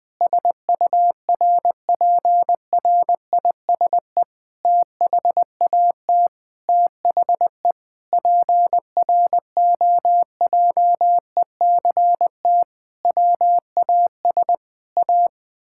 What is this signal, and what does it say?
Morse code, heard as a keyed tone, about 20 wpm, 700 Hz. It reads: SURPRISE THAT THE PROJECT WAS A
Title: Sentences with 8-Word Phrases - Spoken Beginning - 20wpm